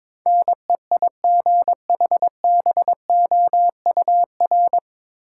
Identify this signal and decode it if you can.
Morse code, heard as a keyed tone, about 22 wpm, 700 Hz. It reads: NEIGHBOUR
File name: 9 Letter Words - First 3 Letters Given - 22wpm